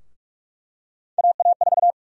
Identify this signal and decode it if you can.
Morse code, heard as a keyed tone, about 43 wpm, 700 Hz. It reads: AA4
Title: USA Call Sign Prefixes - 45wpm